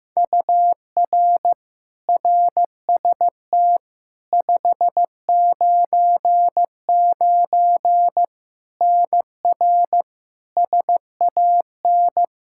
Morse code, heard as a keyed tone, about 15 wpm, 700 Hz. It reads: UR RST 599 NR SAN
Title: Practice QSO - Exchange 1 - Beginning - 15wpm